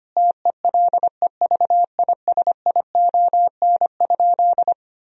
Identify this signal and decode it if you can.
Morse code, heard as a keyed tone, about 25 wpm, 700 Hz. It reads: TELEVISION?
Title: Sentences from Top 1000 Words - Review Part 4 - 25wpm